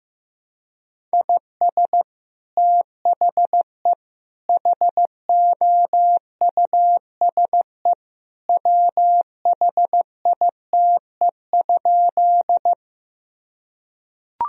Morse code, heard as a keyed tone, about 15 wpm, 700 Hz. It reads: IS THE HOUSE WHITE?
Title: Sentences from Top 1000 Words - Review Part 1 - 15wpm